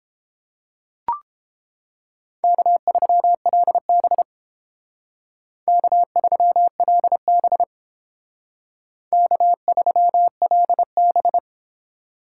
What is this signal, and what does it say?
Morse code, heard as a keyed tone, about 28 wpm, 700 Hz. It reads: K3LB K3LB K3LB